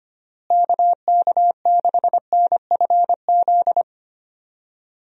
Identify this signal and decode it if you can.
Morse code, heard as a keyed tone, about 25 wpm, 700 Hz. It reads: KK6NFZ
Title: USA Call Signs - 6 Characters - Repeated 2x - 25wpm